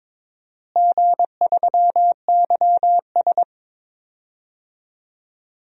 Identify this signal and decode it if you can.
Morse code, heard as a keyed tone, about 22 wpm, 700 Hz. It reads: G3YS
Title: DXCC-100 Call Signs - 4 Characters - Repeated 2x - 22wpm